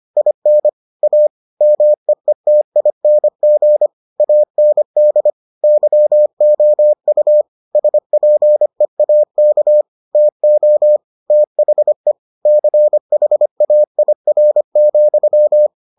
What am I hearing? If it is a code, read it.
Morse code, heard as a keyed tone, about 25 wpm, 600 Hz. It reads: IN A MEETING AND YOU SPEAK TO THE CHAIR,